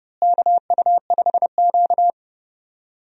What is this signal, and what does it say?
Morse code, heard as a keyed tone, about 32 wpm, 700 Hz. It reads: KU5Q
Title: USA Call Signs - 4 Characters - Repeated 3x - 30wpm